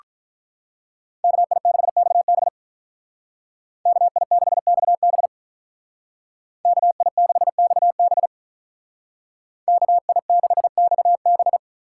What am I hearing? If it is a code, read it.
Morse code, heard as a keyed tone, about 43 wpm, 700 Hz. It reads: KI6XB KI6XB KI6XB KI6XB
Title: Speed Racing - USA 2x2 Call Signs 35wpm